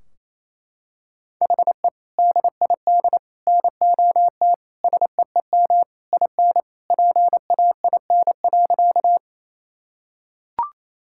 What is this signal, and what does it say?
Morse code, heard as a keyed tone, about 27 wpm, 700 Hz. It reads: HE DID NOT SEEM IN PAIN.
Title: Sentences from Top 1600 Words 28wpm